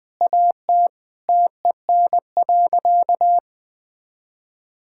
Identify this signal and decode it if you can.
Morse code, heard as a keyed tone, about 21 wpm, 700 Hz. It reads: AT TEN.
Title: Sentences with 7-Word Phrases - 20wpm